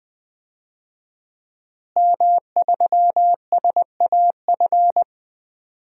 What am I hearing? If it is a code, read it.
Morse code, heard as a keyed tone, about 20 wpm, 700 Hz. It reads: M3SAF
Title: DXCC-100 Call Signs - 5 Characters - Repeated 3x - 20wpm